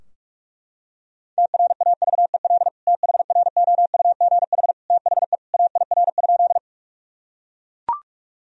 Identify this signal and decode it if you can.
Morse code, heard as a keyed tone, about 41 wpm, 700 Hz. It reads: TRAVEL THROUGH THE AIR?
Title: Sentences from Top 1000 Words - Review Part 2 - 45wpm